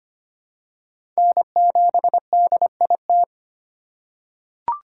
Morse code, heard as a keyed tone, about 25 wpm, 700 Hz. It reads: N7DIT
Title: USA Call Signs - 5 Characters - 2x Character Spacing - 25wpm